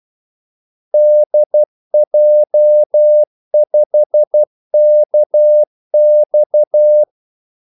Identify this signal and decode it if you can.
Morse code, heard as a keyed tone, about 12 wpm, 600 Hz. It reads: DJ5KX
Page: 2 ekstra ordmellemrum mellem kaldesignalerne